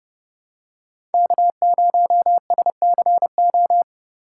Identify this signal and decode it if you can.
Morse code, heard as a keyed tone, about 29 wpm, 700 Hz. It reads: K0SCO